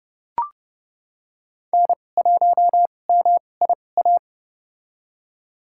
Morse code, heard as a keyed tone, about 30 wpm, spaced as for 22 wpm, 700 Hz. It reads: N1MIA
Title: USA Call Signs - 2x Character Spacing - 30wpm